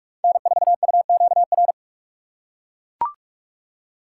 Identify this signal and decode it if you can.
Morse code, heard as a keyed tone, about 41 wpm, 700 Hz. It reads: N4UQR
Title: USA Call Signs - 5 Characters - 4x Character Spacing - 45wpm